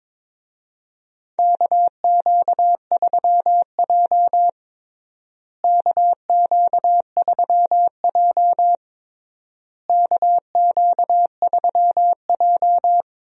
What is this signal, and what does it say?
Morse code, heard as a keyed tone, about 22 wpm, 700 Hz. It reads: KQ3J KQ3J KQ3J
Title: USA Call Signs - 4 Characters - Repeated 3x - 22wpm